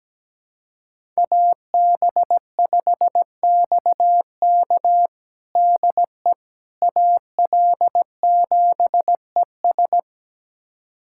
Morse code, heard as a keyed tone, about 17 wpm, 700 Hz. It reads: AB5XK DE AL7ES
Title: USA Callsign DE Callsign - 5 Characters - 17wpm